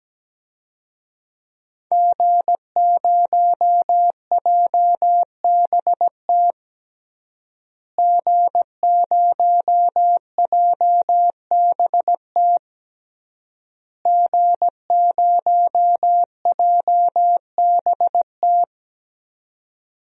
Morse code, heard as a keyed tone, about 17 wpm, 700 Hz. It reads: G0JBT G0JBT G0JBT
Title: DXCC-100 Call Signs - 5 Characters - Repeated 3x - 17wpm